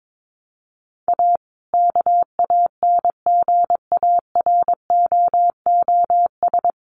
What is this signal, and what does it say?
Morse code, heard as a keyed tone, about 22 wpm, 700 Hz. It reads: A KANGAROOS
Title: Easy Phrases 22wpm